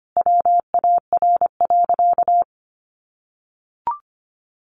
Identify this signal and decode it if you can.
Morse code, heard as a keyed tone, about 25 wpm, 700 Hz. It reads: WAR.
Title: Sentences from Top 1200 Words 25wpm